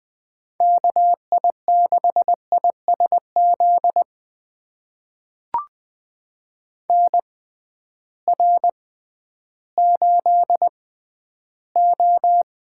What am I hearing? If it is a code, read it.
Morse code, heard as a keyed tone, about 20 wpm, spaced as for 12 wpm, 700 Hz. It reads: KI6ISZ NR8O
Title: USA Call Signs - 6x Character Spacing - 20wpm